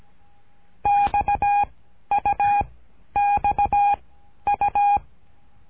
Morse code, heard as a keyed tone, about 17 wpm, 820 Hz. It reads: X U X U
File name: cursos / CURSO DE TELEGRAFIA / Telegrafia - Trilha 13